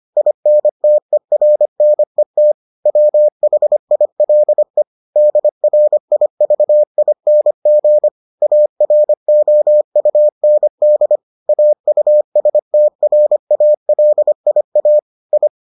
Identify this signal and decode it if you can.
Morse code, heard as a keyed tone, about 25 wpm, 600 Hz. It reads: INTERNET WHILE DRIVING AROUND AUSTRALIA I